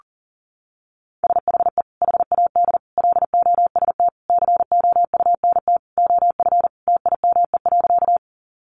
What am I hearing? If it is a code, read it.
Morse code, heard as a keyed tone, about 40 wpm, 700 Hz. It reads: SHE HAD LOST COUNT OF TIME.
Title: Sentences from Top 1500 Words 40wpm